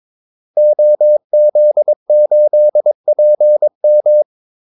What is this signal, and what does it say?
Morse code, heard as a keyed tone, about 22 wpm, 600 Hz. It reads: OZ8PM
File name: Callsigns-22wpm